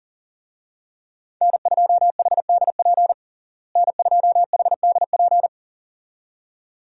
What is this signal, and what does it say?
Morse code, heard as a keyed tone, about 40 wpm, 700 Hz. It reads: N2HDP N2HDP
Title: USA Call Signs - 5 Characters - Repeated 2x - 40wpm